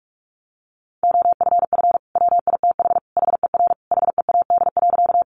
Morse code, heard as a keyed tone, about 44 wpm, 700 Hz. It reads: OFF WITH HER HEAD.
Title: Sentences from Top 300 Words 45wpm